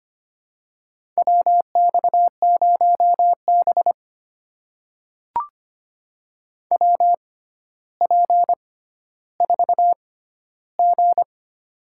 Morse code, heard as a keyed tone, about 25 wpm, spaced as for 13 wpm, 700 Hz. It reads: WX0B WP4G
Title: USA Call Signs - 4 Characters - 6x Character Spacing - 25wpm